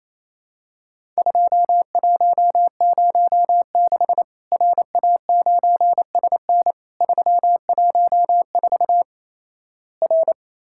Morse code, heard as a keyed tone, about 28 wpm, 700 Hz. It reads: 2106 RA9SN 314 R
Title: Worked All Europe DX Contest - QTC Exchange - 28wpm